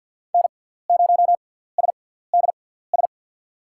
Morse code, heard as a keyed tone, about 47 wpm, spaced as for 17 wpm, 700 Hz. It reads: N0SDS
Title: USA Call Signs - 5 Characters - 6x Character Spacing - 50wpm